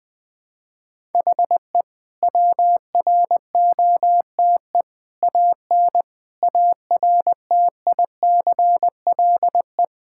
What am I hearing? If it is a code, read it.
Morse code, heard as a keyed tone, about 21 wpm, 700 Hz. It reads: HE WROTE AN ARTICLE
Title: Sentences with 5-Word Phrases - 20wpm